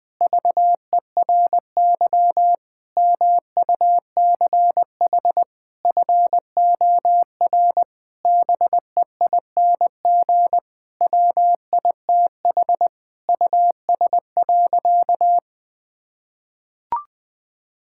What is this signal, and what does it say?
Morse code, heard as a keyed tone, about 20 wpm, 700 Hz. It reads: VERY MUCH FOR BEING WITH US.
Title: Sentences with 5-Word Phrases - 20wpm